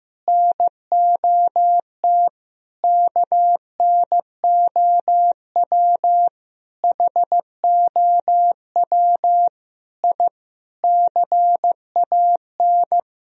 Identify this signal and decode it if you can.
Morse code, heard as a keyed tone, about 15 wpm, 700 Hz. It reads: NOT KNOW HOW I CAN
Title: Sentences from Top 1000 Words - Review Part 2 - 15wpm